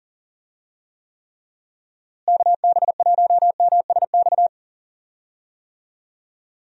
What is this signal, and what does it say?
Morse code, heard as a keyed tone, about 40 wpm, 700 Hz. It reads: KB1MSX
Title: USA Call Signs - 6 Characters - Repeated 3x - 40wpm